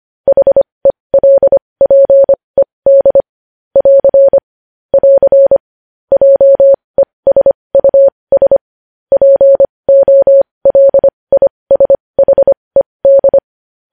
Morse code, heard as a keyed tone, about 25 wpm, spaced as for 20 wpm, 550 Hz. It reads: HELPED <AR> <AR> JESUS POLISHED
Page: news-headlines-in-morse-code-at-20-wpm